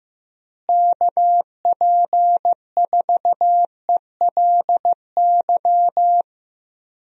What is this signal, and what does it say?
Morse code, heard as a keyed tone, about 16 wpm, 700 Hz. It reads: KP4ELY
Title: USA Call Signs - 6 Characters - Repeated 3x - 15wpm